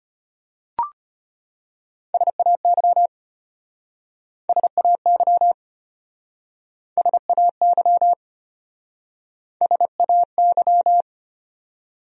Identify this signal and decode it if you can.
Morse code, heard as a keyed tone, about 32 wpm, 700 Hz. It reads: SAY SAY SAY SAY